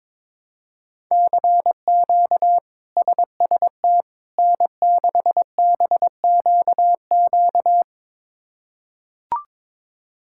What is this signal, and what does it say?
Morse code, heard as a keyed tone, about 22 wpm, 700 Hz. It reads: CQ SST N6BQQ
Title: K1USN SST QSO Practice - 22wpm